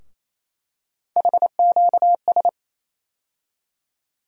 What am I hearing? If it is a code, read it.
Morse code, heard as a keyed tone, about 27 wpm, 700 Hz. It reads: HQS
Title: USA Call Sign Postfixes - 3 Letters 28wpm